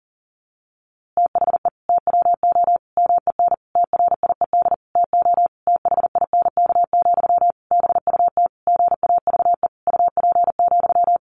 Morse code, heard as a keyed tone, about 40 wpm, 700 Hz. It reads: THE TWO MEN TRIED TO THINK, BUT GAVE UP,
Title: A Sound Of Thunder 40wpm